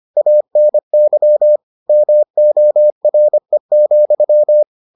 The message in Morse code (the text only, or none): ANY MORE,